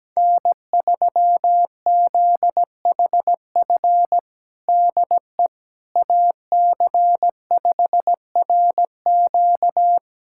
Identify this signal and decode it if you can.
Morse code, heard as a keyed tone, about 17 wpm, 700 Hz. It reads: N3ZHF DE AC5RQ
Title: USA Callsign DE Callsign - 5 Characters - 17wpm